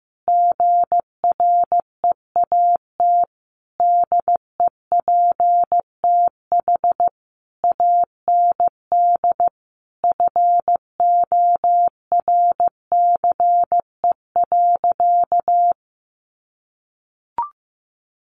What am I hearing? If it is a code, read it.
Morse code, heard as a keyed tone, about 15 wpm, 700 Hz. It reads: GREAT DEPTH AND FORCE.
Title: Sentences from Top 1200 Words 15wpm